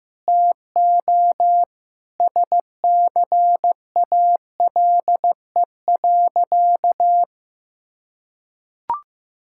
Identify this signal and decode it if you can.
Morse code, heard as a keyed tone, about 15 wpm, 700 Hz. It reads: TO SCALE.
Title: Sentences from Top 1000 Words - Review Part 5 - 15wpm